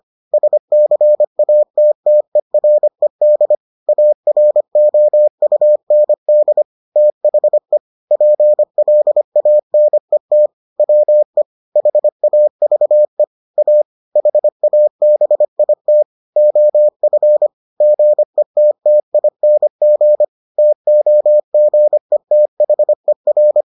Morse code, heard as a keyed tone, about 25 wpm, 600 Hz. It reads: SCATTERED AROUND THE PLANET WE HAVE A HABIT OF GETTING TOGETHER